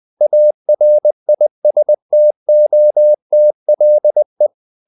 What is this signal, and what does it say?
Morse code, heard as a keyed tone, about 20 wpm, 600 Hz. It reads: ARISTOTLE
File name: CW / Practice / EnglishQuotations0320